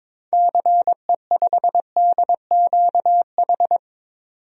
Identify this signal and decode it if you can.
Morse code, heard as a keyed tone, about 22 wpm, 700 Hz. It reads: CE5DQH
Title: DXCC-100 Call Signs - 6 Characters - 22wpm